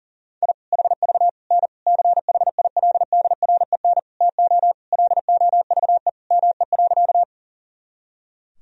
Morse code, heard as a keyed tone, about 40 wpm, 700 Hz. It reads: I HV N CHILDREN TO LOVE ME.
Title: Sentences from Top 400 Words and Abbreviations 40wpm